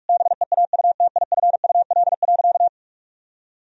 Morse code, heard as a keyed tone, about 46 wpm, 700 Hz. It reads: BEAUTIFUL.
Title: Sentences from Top 1000 Words - Review Part 4 - 45wpm